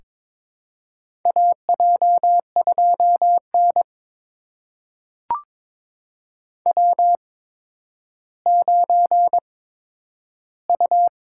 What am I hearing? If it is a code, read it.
Morse code, heard as a keyed tone, about 22 wpm, spaced as for 11 wpm, 700 Hz. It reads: AJ2N W9U
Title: USA Call Signs - 4 Characters - 8x Character Spacing - 22wpm